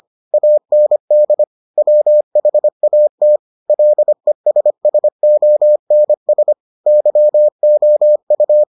Morse code, heard as a keyed tone, about 25 wpm, 600 Hz. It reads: AND WHAT LESSONS YOU